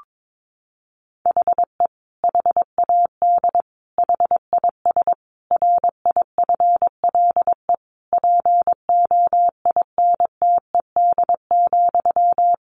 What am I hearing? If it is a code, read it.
Morse code, heard as a keyed tone, about 22 wpm, 700 Hz. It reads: HE HAD HIS RIFLE POINTED,
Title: A Sound Of Thunder 22wpm